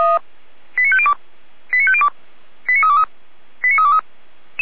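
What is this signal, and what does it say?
Five-tone selective call (ZVEI), heard as quick runs of five tones, tones of about 70 ms. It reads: 89502 89502 89323 89323
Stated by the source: ZVEI five-tone selective call 89001, received on an SDR, from the feuerwehr-orlamuende.org fire brigade audio archive.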